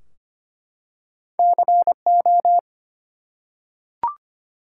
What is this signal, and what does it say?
Morse code, heard as a keyed tone, about 25 wpm, 700 Hz. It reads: CO
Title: US States and Canadian Provinces 25wpm